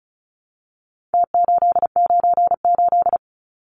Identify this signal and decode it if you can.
Morse code, heard as a keyed tone, about 34 wpm, 700 Hz. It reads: T898
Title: Numbers - Contest Style - 4-Digit 35wpm